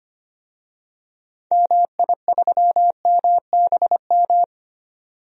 Morse code, heard as a keyed tone, about 25 wpm, 700 Hz. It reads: MI3MBM
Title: DXCC-100 Call Signs - 6 Characters - 25wpm